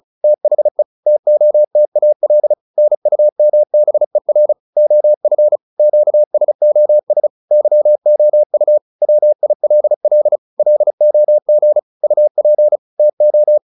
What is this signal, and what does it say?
Morse code, heard as a keyed tone, about 34 wpm, 600 Hz. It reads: THE TOTAL NUMBER OF QSOS YOU WILL LOG UP TO